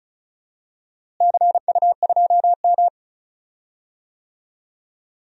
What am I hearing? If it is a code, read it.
Morse code, heard as a keyed tone, about 36 wpm, 700 Hz. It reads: CU2M
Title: DXCC-100 Call Signs - 4 Characters - Repeated 2x - 35wpm